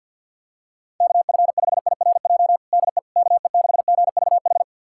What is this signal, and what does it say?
Morse code, heard as a keyed tone, about 50 wpm, 690 Hz. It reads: KF5IRJ DE KE6GVH
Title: USA Callsign DE Callsign - 6 Characters - 50wpm